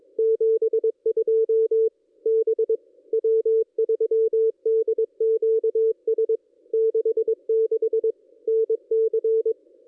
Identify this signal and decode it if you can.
Morse code, heard as a keyed tone, about 21 wpm, 440 Hz. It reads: 72 B W3DQS 66 NC
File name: contests / ss-cw-k5tr-op-n5rz -1400